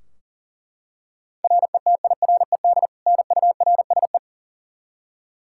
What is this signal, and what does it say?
Morse code, heard as a keyed tone, about 40 wpm, 700 Hz. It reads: RETIRED NURSE
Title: Practice QSO - Exchange 3 - More Info - 40wpm